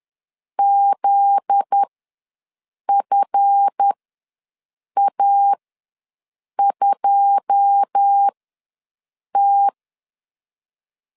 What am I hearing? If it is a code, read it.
Morse code, heard as a keyed tone, about 10 wpm, spaced as for 6 wpm, 800 Hz. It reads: ZFA2T
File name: LEZIONI / 9 2 ws 10wpm 4130